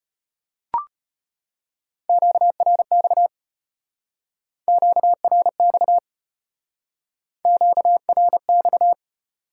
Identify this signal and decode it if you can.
Morse code, heard as a keyed tone, about 32 wpm, 700 Hz. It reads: QRX QRX QRX